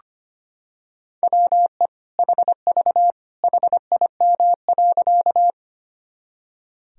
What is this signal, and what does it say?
Morse code, heard as a keyed tone, about 25 wpm, 700 Hz. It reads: WE HV HIM.
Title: Sentences from Top 100 Words and Abbreviations 25wpm